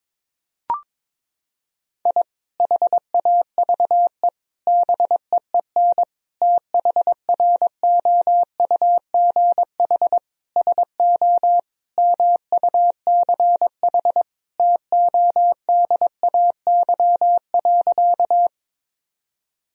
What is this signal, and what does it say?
Morse code, heard as a keyed tone, about 22 wpm, 700 Hz. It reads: I HAVE BEEN THROUGH SO MUCH TODAY.
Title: Sentences from Top 1000 Words - Review Part 4 - 22wpm